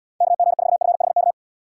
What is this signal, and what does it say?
Morse code, heard as a keyed tone, about 78 wpm, 700 Hz. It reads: KG4FHK